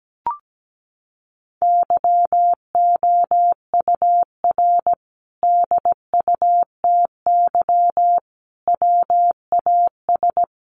Morse code, heard as a keyed tone, about 17 wpm, 700 Hz. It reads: YOUR DUTY WAS
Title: Sentences from Top 1500 Words 17wpm